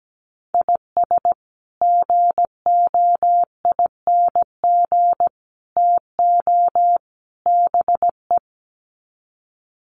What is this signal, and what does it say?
Morse code, heard as a keyed tone, about 17 wpm, 700 Hz. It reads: IS GOING TO BE
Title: Sets of 4 Words 17wpm